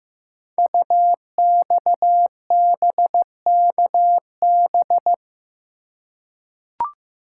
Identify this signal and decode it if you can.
Morse code, heard as a keyed tone, about 15 wpm, 700 Hz. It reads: UXBKB
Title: Code Groups - 5 Letters - 15wpm